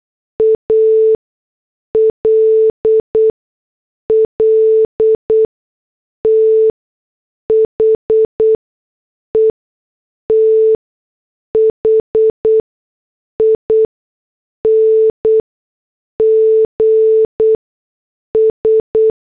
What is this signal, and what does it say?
Morse code, heard as a keyed tone, about 8 wpm, spaced as for 6 wpm, 440 Hz. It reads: ALL THE THINGS
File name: morse / headlines